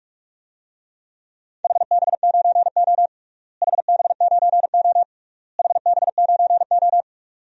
Morse code, heard as a keyed tone, about 41 wpm, 700 Hz. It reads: HB9O HB9O HB9O
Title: DXCC-100 Call Signs - 4 Characters - Repeated 3x - 45wpm